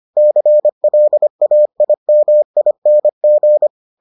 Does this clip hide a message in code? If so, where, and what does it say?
Yes: Morse code from 0.2 to 3.7 s, heard as a keyed tone, about 25 wpm, 600 Hz. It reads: CLAIMING